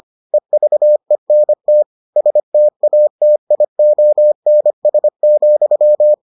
Morse code, heard as a keyed tone, about 25 wpm, 600 Hz. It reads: EVENT STATIONS,